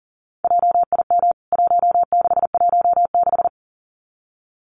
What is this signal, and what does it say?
Morse code, heard as a keyed tone, about 40 wpm, 700 Hz. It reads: JIM 1616
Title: CWT Exchange 40wpm